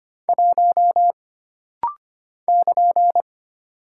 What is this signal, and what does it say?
Morse code, heard as a keyed tone, about 25 wpm, 700 Hz. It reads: 1 <KN>
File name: Single Character (Letter:Number:Punctuation:Prosign) - Rapid-Fire - 25wpm